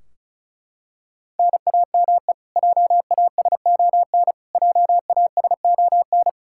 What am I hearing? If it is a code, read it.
Morse code, heard as a keyed tone, about 34 wpm, 700 Hz. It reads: NAME JASON JASON
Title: K1USN SST QSO Practice 35wpm